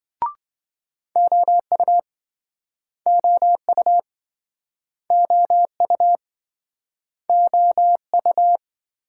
Morse code, heard as a keyed tone, about 28 wpm, 700 Hz. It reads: OU OU OU OU